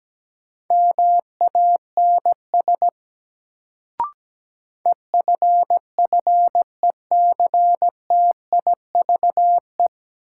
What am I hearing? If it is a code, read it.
Morse code, heard as a keyed tone, about 17 wpm, 700 Hz. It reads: MANS EFFECTIVE
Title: Sentences from Top 1000 Words - Review Part 6 - New Word Drill - 17wpm